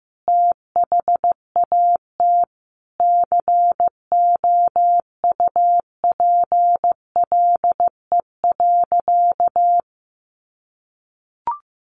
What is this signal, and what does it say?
Morse code, heard as a keyed tone, about 15 wpm, 700 Hz. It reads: THAT COUPLE.
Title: Sentences from Top 1100 Words 15wpm